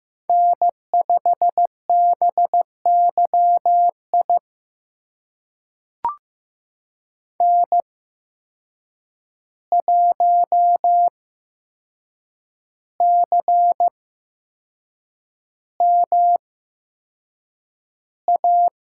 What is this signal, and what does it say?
Morse code, heard as a keyed tone, about 15 wpm, spaced as for 6 wpm, 700 Hz. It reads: N5BYI N1CMA
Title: USA Call Signs - 5 Characters - 8x Character Spacing - 15wpm